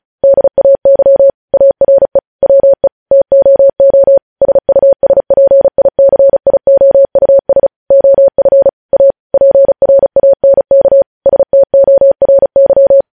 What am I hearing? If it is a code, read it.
Morse code, heard as a keyed tone, about 36 wpm, 550 Hz. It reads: DAY ARE WE TOO SUSPICIOUS OF A PRANK STORY
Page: news headlines in morse code at 35 wpm